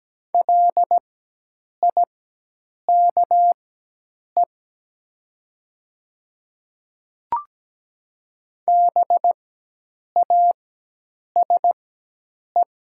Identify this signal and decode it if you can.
Morse code, heard as a keyed tone, about 17 wpm, spaced as for 8 wpm, 700 Hz. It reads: LIKE BASE
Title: 4x Character Spacing - Top 300 Words - 17wpm